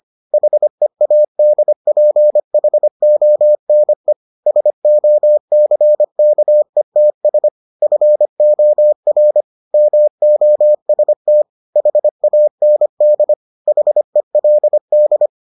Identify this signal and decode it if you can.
Morse code, heard as a keyed tone, about 25 wpm, 600 Hz. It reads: HEADPHONE SOCKETS FOR MOST HAND HELD